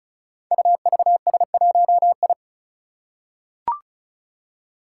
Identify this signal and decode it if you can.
Morse code, heard as a keyed tone, about 36 wpm, 700 Hz. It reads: UVS1I
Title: Five Challenging Letter-Number Combinations 35wpm